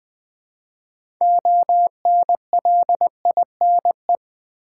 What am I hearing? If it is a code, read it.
Morse code, heard as a keyed tone, about 20 wpm, 700 Hz. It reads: ONLINE
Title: Build A Word - 2 Letters At A Time - 6-Letter Words - 20wpm